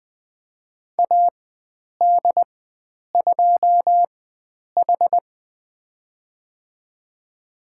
Morse code, heard as a keyed tone, about 20 wpm, spaced as for 9 wpm, 700 Hz. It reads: AD2H
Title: USA Call Signs - 4 Characters - 4x Character Spacing - 20wpm